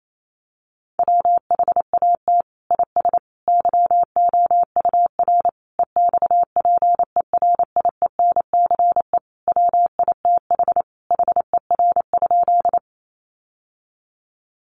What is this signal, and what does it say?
Morse code, heard as a keyed tone, about 28 wpm, 700 Hz. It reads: WHAT IS YOUR EXPERIENCE WITH HER?
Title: Sentences from Top 600 Words 28wpm